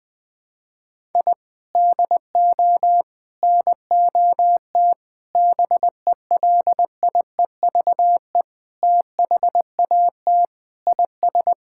Morse code, heard as a keyed tone, about 20 wpm, 700 Hz. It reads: I DO NOT BELIEVE THAT IS
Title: Sentences with 4-Word Phrases - 20wpm